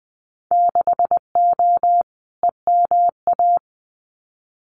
Morse code, heard as a keyed tone, about 20 wpm, 700 Hz. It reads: 6O EMA
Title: Winter Field Day Exchanges Encore 20wpm